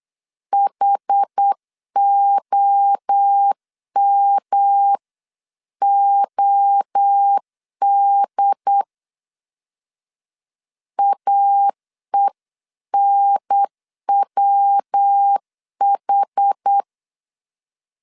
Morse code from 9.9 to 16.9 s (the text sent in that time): AENWH